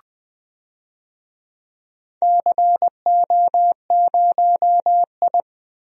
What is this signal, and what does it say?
Morse code, heard as a keyed tone, about 21 wpm, 700 Hz. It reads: CO0I
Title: DXCC-100 Call Signs - 4 Characters - 20wpm